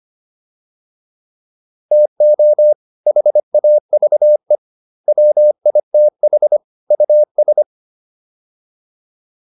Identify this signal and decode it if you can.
Morse code, heard as a keyed tone, about 25 wpm, 600 Hz. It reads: TO HAVE WITH US